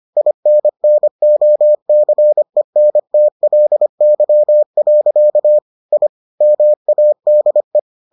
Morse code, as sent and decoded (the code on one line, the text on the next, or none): .. -. -. --- -.-. -. - .-.. -.-- .-.-.- / .. / -- .- -.. .
INNOCENTLY. I MADE